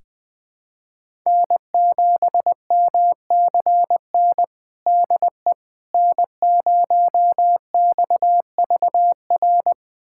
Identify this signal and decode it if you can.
Morse code, heard as a keyed tone, about 21 wpm, 700 Hz. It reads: N7MCN DE N0XVR
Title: USA Callsign DE Callsign - 5 Characters - 20wpm